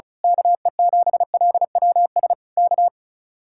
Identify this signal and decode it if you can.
Morse code, heard as a keyed tone, about 33 wpm, 700 Hz. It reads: KE7LWS K
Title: Speed-Racing Common QSO Phrases - 35wpm